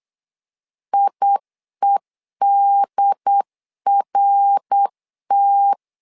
Morse code, heard as a keyed tone, about 8 wpm, 800 Hz. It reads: IEDRT
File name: LEZIONI / 3 MWRO / 3 MWR 8wpm 2444